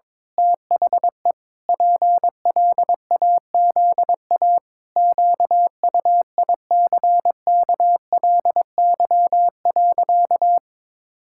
Morse code, heard as a keyed tone, about 22 wpm, 700 Hz. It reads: THE PLAZA QUICKLY.
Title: Pangram Challenge v2 - 22wpm